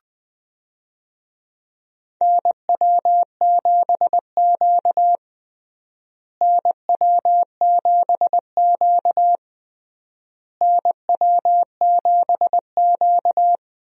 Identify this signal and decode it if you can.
Morse code, heard as a keyed tone, about 20 wpm, 700 Hz. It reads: NW7Q NW7Q NW7Q
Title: USA Call Signs - 4 Characters - Repeated 3x - 20wpm